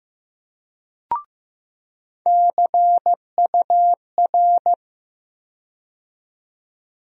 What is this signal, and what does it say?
Morse code, heard as a keyed tone, about 15 wpm, 700 Hz. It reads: CUR
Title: USA Call Sign Postfixes - 3 Letters 15wpm